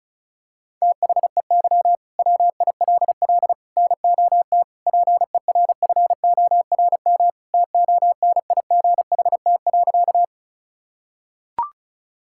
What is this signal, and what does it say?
Morse code, heard as a keyed tone, about 34 wpm, 700 Hz. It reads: THEY WILL NOT PERFORM TONIGHT.
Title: Sentences from Top 1000 Words - Review Part 5 - 35wpm